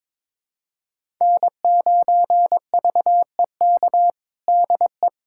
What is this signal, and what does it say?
Morse code, heard as a keyed tone, about 22 wpm, 700 Hz. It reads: N9VEK DE
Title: USA Callsign DE Callsign 22wpm